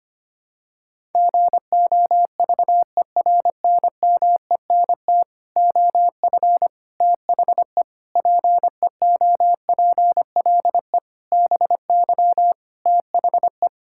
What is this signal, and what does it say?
Morse code, heard as a keyed tone, about 25 wpm, 700 Hz. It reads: GOVERNMENT OF THE PEOPLE BY THE
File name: Sets of 9 Words - 25wpm